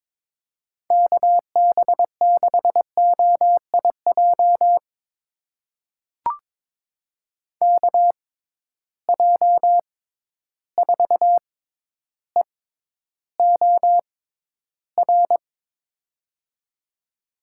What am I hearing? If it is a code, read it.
Morse code, heard as a keyed tone, about 22 wpm, spaced as for 11 wpm, 700 Hz. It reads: KB6OIJ KJ4EOR
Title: USA Call Signs - 6 Characters - 6x Character Spacing - 22wpm